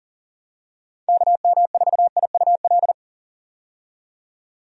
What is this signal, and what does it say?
Morse code, heard as a keyed tone, about 40 wpm, 700 Hz. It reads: KM4IUL
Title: USA Call Signs - 6 Characters - 40wpm